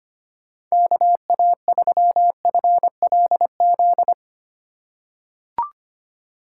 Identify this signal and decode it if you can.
Morse code, heard as a keyed tone, about 25 wpm, 700 Hz. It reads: KA3FLZ
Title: USA Call Signs - 6 Characters - 2x Character Spacing - 25wpm